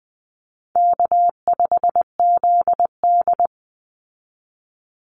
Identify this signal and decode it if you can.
Morse code, heard as a keyed tone, about 20 wpm, 700 Hz. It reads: K5ZD
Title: Easy Call Signs - 20wpm